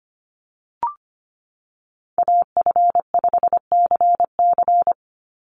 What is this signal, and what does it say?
Morse code, heard as a keyed tone, about 25 wpm, 700 Hz. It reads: AF5CC
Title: Call Signs Repeated 3x - 25wpm